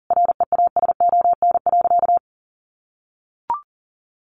Morse code, heard as a keyed tone, about 40 wpm, 700 Hz. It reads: REASON.
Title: Sentences from Top 700 Words 40wpm